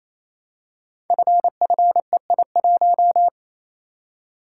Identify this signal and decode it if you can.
Morse code, heard as a keyed tone, about 27 wpm, 700 Hz. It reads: FFEI1
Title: Code Groups - 5 Letter:Numbers - 28wpm